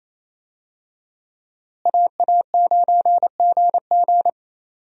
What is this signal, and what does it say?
Morse code, heard as a keyed tone, about 28 wpm, 700 Hz. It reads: AA9GG
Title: USA Call Signs - 5 Characters - Repeated 2x - 28wpm